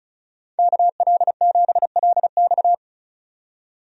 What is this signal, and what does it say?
Morse code, heard as a keyed tone, about 36 wpm, 700 Hz. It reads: KL7LX
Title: DXCC-100 Call Signs - 5 Characters - 35wpm